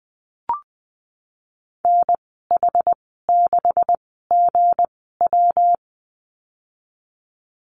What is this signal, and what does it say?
Morse code, heard as a keyed tone, about 20 wpm, spaced as for 15 wpm, 700 Hz. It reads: NH6GW
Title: 193 Lesson 31g - Letter v - Call Signs 20wpm